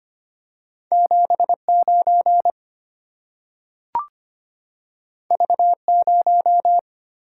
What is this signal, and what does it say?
Morse code, heard as a keyed tone, about 25 wpm, 690 Hz. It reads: 79 V0
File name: Two Letter-Number Combinations - 25wpm